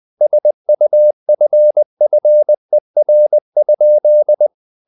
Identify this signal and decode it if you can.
Morse code, heard as a keyed tone, about 20 wpm, 600 Hz. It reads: SUFFER?
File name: the old man and the sea part 2 20wpm